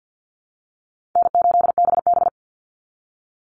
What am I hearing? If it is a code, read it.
Morse code, heard as a keyed tone, about 50 wpm, 700 Hz. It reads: N8BB
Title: NA SOTA Chaser Callsigns 50wpm